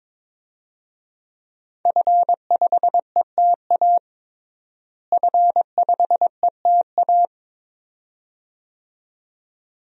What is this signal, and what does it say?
Morse code, heard as a keyed tone, about 21 wpm, 700 Hz. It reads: F5ETA F5ETA
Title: DXCC-100 Call Signs - 5 Characters - Repeated 2x - 22wpm